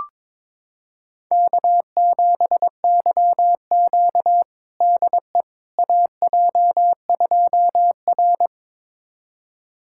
Morse code, heard as a keyed tone, about 21 wpm, 700 Hz. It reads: K7YQ DE AJ2R
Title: USA Callsign DE Callsign - 4 Characters - 22wpm